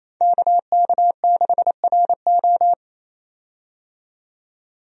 Morse code, heard as a keyed tone, about 28 wpm, 700 Hz. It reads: KK6RO
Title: USA Call Signs - 5 Characters - 28wpm